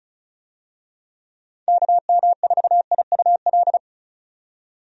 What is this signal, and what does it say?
Morse code, heard as a keyed tone, about 36 wpm, 700 Hz. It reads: KM4IUL
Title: USA Call Signs - 6 Characters - 35wpm